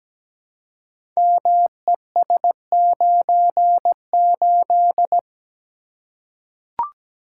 Morse code, heard as a keyed tone, about 17 wpm, 700 Hz. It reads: MES98
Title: Code Groups - 5 Letter:Numbers - 17wpm